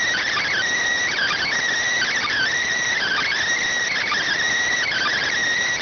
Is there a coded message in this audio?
Yes: Emergency Alert System (SAME) tones from 1.1 to 5.3 s, heard as harsh buzzing bursts.